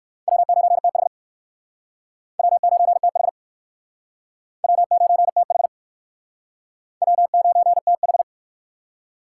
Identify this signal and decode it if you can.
Morse code, heard as a keyed tone, about 60 wpm, 700 Hz. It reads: W9TH W9TH W9TH W9TH